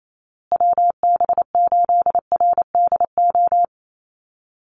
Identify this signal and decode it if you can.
Morse code, heard as a keyed tone, about 28 wpm, 700 Hz. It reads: WB8RDO
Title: USA Call Signs Repeated 2x 28wpm